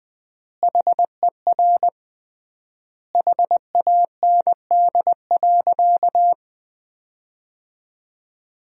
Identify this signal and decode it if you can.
Morse code, heard as a keyed tone, about 20 wpm, 700 Hz. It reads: HER HAND.
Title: 3x Word-Spacing - Sentences from Top 400 Words - 20wpm